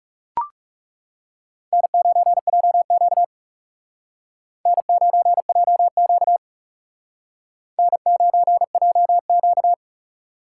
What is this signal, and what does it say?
Morse code, heard as a keyed tone, about 37 wpm, 700 Hz. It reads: N9JQ N9JQ N9JQ
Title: Speed Racing - USA 1x2 Call Signs 30wpm